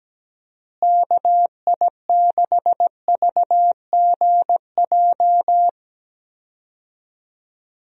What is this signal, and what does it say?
Morse code, heard as a keyed tone, about 17 wpm, 700 Hz. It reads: KI6VGJ
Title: USA Call Signs - 6 Characters - Repeated 2x - 17wpm